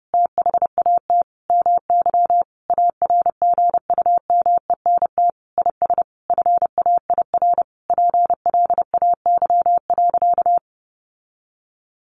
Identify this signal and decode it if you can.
Morse code, heard as a keyed tone, about 30 wpm, 700 Hz. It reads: THAT MY ARGUMENT IS FAIR PLAY.
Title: Sentences from Top 1400 Words 30wpm